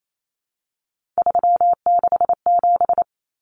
Morse code, heard as a keyed tone, about 28 wpm, 700 Hz. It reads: 367
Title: Common Cut Numbers - 3-Digit 28wpm